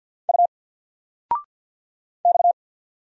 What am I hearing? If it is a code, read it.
Morse code, heard as a keyed tone, about 50 wpm, 700 Hz. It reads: U X
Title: Single Letter-Number 50wpm